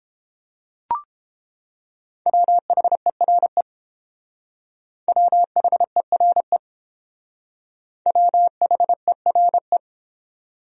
Morse code, heard as a keyed tone, about 29 wpm, 700 Hz. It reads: WHERE WHERE WHERE